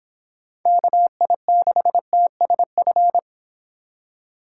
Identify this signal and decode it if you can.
Morse code, heard as a keyed tone, about 26 wpm, 700 Hz. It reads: KI6TSF